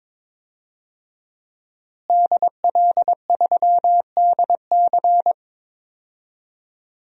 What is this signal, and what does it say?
Morse code, heard as a keyed tone, about 22 wpm, 700 Hz. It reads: DL3DC
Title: DXCC-100 Call Signs - 5 Characters - Repeated 3x - 22wpm